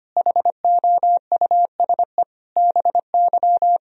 Morse code, heard as a keyed tone, about 25 wpm, 700 Hz. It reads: HOUSE BY
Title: Sentences from Top 1000 Words - Review Part 4 - 25wpm